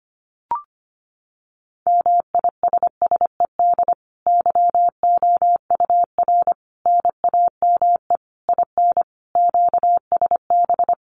Morse code, heard as a keyed tone, about 25 wpm, 700 Hz. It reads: MISSED YOUR NAME IN QSB
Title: Common QSO Phrases 25wpm